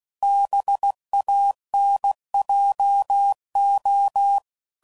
Morse code, heard as a keyed tone, about 16 wpm, 800 Hz. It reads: BANJO